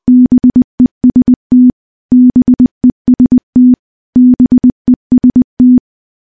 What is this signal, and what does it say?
Morse code, heard as a keyed tone, about 20 wpm, 250 Hz. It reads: BEST BEST BEST
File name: morse-training-20WPM